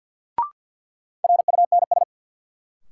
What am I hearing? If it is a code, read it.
Morse code, heard as a keyed tone, about 52 wpm, 700 Hz. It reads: RUNS
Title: Sentences from Top 600 Words - New Word Drill - 50wpm